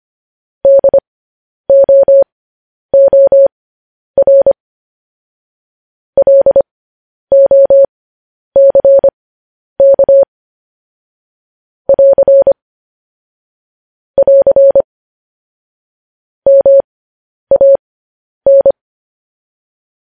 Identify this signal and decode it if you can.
Morse code, heard as a keyed tone, about 25 wpm, spaced as for 10 wpm, 550 Hz. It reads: DOOR LOCK <AR> <AR> MAN
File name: podcast / News headlines-1x-25x10